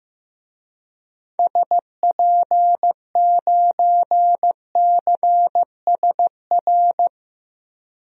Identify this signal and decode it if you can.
Morse code, heard as a keyed tone, about 15 wpm, 700 Hz. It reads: SP9CSR
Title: DXCC-100 Call Signs - 6 Characters - 15wpm